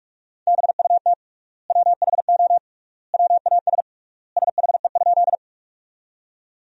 Morse code, heard as a keyed tone, about 46 wpm, 700 Hz. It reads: BUT WHO WAS SHE?
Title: 3x Word-Spacing - Sentences from Top 100 Words - 45wpm